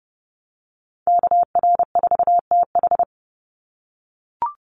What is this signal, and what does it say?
Morse code, heard as a keyed tone, about 30 wpm, 700 Hz. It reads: KR4TH
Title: Call Signs - 30wpm